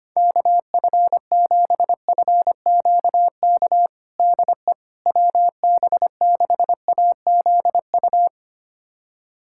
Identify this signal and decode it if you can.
Morse code, heard as a keyed tone, about 25 wpm, 700 Hz. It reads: KF7FQK DE WB6AZU
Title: USA Callsign DE Callsign - 6 Characters - 25wpm